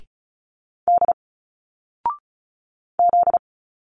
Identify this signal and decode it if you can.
Morse code, heard as a keyed tone, about 34 wpm, 700 Hz. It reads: D Z
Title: Single Letters 35wpm